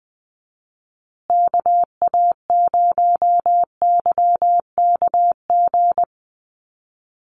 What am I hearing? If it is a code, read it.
Morse code, heard as a keyed tone, about 20 wpm, 700 Hz. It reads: KA0YKG
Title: USA Call Signs Repeated 3x 20wpm